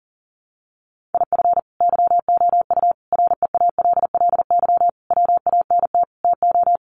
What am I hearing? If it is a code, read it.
Morse code, heard as a keyed tone, about 40 wpm, 700 Hz. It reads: IF YOU REALLY WANT TO
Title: Sets of 5 Words Encore2 40wpm